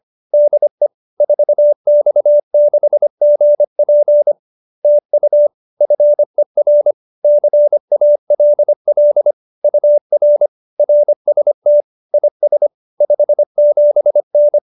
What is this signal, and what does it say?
Morse code, heard as a keyed tone, about 25 wpm, 590 Hz. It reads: DE 4X6GP TU FER CALL UR RST IS 57N
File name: QSO310 25